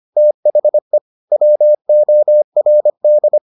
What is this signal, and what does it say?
Morse code, heard as a keyed tone, about 25 wpm, 600 Hz. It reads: THE WORD